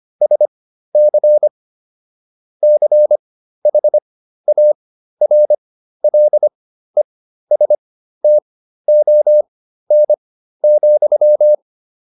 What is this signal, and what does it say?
Morse code, heard as a keyed tone, about 25 wpm, spaced as for 13 wpm, 600 Hz. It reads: SC CHARLESTON,